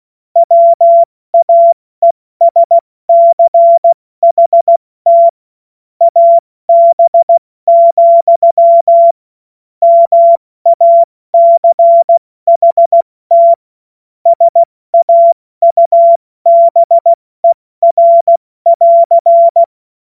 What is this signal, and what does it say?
Morse code, heard as a keyed tone, about 16 wpm, 680 Hz. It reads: WAESCHT AB, MACHT SAUBER <AR>
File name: Morsekurs / Witze / Witz 01. Februar 2022 70 F80